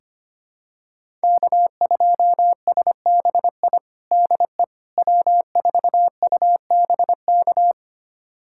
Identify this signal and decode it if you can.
Morse code, heard as a keyed tone, about 25 wpm, 700 Hz. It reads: K2SBI DE W4UBK
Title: USA Callsign DE Callsign - 5 Characters - 25wpm